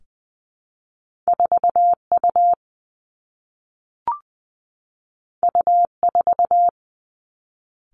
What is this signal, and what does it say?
Morse code, heard as a keyed tone, about 20 wpm, 700 Hz. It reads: 4 U U 4
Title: Two Challenging Letter-Number Combinations 20wpm